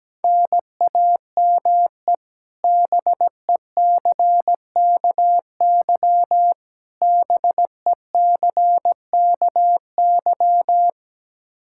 Morse code, heard as a keyed tone, about 17 wpm, 700 Hz. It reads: NAME BECKY BECKY
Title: K1USN SST QSO Practice 17wpm